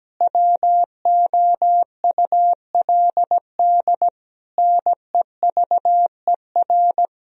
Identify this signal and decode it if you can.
Morse code, heard as a keyed tone, about 17 wpm, 700 Hz. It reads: WOULD NEVER
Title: Sentences from Top 1000 Words - Review Part 2 - 17wpm